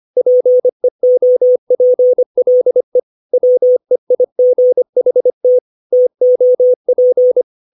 Morse code, heard as a keyed tone, about 25 wpm, 500 Hz. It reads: PEOPLE WEIGHT TOP
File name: 632 words / 632 Words 25WPM